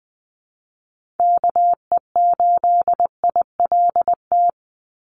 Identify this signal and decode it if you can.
Morse code, heard as a keyed tone, about 19 wpm, 700 Hz. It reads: KE8ILT
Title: USA Call Signs 20wpm